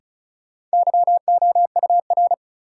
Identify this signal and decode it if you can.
Morse code, heard as a keyed tone, about 34 wpm, 700 Hz. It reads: YOUR
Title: Sentences from Top 100 Words - New Word Drill - 35wpm